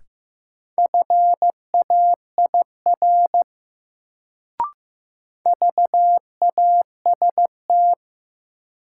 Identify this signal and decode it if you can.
Morse code, heard as a keyed tone, about 15 wpm, 700 Hz. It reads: FAIR VAST
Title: Sentences from Top 1300 Words - New Word Drill Alt - 15wpm